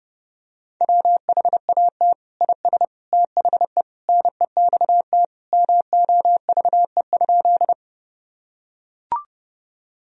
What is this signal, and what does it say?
Morse code, heard as a keyed tone, about 32 wpm, 700 Hz. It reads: WHAT IS THE NEXT MOVE?
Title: Sentences from Top 1000 Words - Review Part 1 - 30wpm